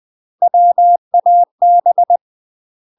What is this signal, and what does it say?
Morse code, heard as a keyed tone, about 20 wpm, 700 Hz. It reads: WAB
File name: UK MASTER QSOs TRAINING / 20WPM / UK MASTER QSOs 0049